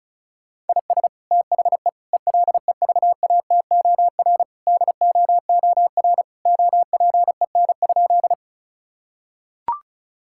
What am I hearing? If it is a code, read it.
Morse code, heard as a keyed tone, about 36 wpm, 700 Hz. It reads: IS THE ELEVATOR DOOR OPEN?
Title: Sentences from Top 1600 Words 35wpm